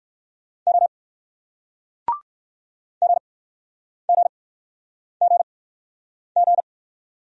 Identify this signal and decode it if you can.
Morse code, heard as a keyed tone, about 60 wpm, spaced as for 9 wpm, 700 Hz. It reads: O GGGG